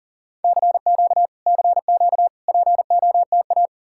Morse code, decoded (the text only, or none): CQ CQ POTA